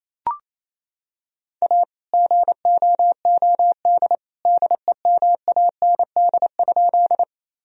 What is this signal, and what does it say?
Morse code, heard as a keyed tone, about 28 wpm, 700 Hz. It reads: A GOOD DEMAND?
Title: Sentences from Top 1000 Words - Review Part 4 - 28wpm